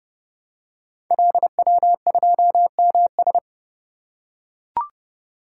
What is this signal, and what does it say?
Morse code, heard as a keyed tone, about 30 wpm, 700 Hz. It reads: LW2MS
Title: Code Groups - 5 Letter:Numbers - 30wpm